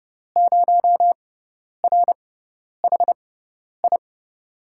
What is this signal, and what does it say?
Morse code, heard as a keyed tone, about 31 wpm, spaced as for 10 wpm, 700 Hz. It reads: 0RHI
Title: USA Call Signs - 5 Characters - 6x Character Spacing - 30wpm